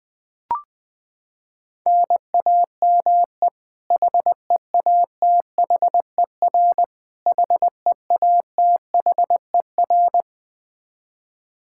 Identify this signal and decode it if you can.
Morse code, heard as a keyed tone, about 21 wpm, 700 Hz. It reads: NAME HEATHER HEATHER
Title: K1USN SST QSO Practice 20wpm